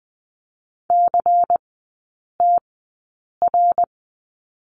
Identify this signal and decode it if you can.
Morse code, heard as a keyed tone, about 20 wpm, spaced as for 8 wpm, 700 Hz. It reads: CTR
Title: 121 Lesson 20e - Number 3 - Three Characters 20wpm